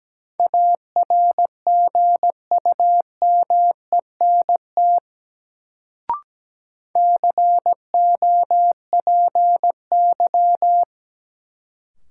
Morse code, heard as a keyed tone, about 17 wpm, 700 Hz. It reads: ARGUMENT COPY
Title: Sentences from Top 1400 Words - New Word Drill - 17wpm